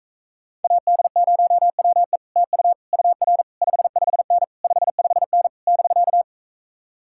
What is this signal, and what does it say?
Morse code, heard as a keyed tone, about 45 wpm, 700 Hz. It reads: AD0WE TU UR 55N 55N <BK>